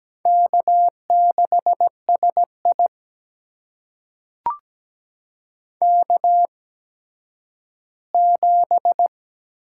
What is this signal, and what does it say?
Morse code, heard as a keyed tone, about 17 wpm, spaced as for 10 wpm, 700 Hz. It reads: K6SI K7
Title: USA Call Signs - 4 Characters - 8x Character Spacing - 17wpm